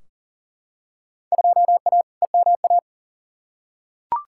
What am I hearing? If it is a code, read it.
Morse code, heard as a keyed tone, about 40 wpm, 700 Hz. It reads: 2A EMA
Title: Field Day Exchanges - 40wpm